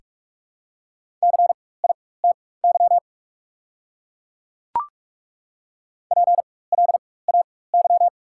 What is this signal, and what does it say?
Morse code, heard as a keyed tone, about 41 wpm, spaced as for 20 wpm, 700 Hz. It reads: CITY PLAY
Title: 4x Character Spacing - Top 300 Words - 45wpm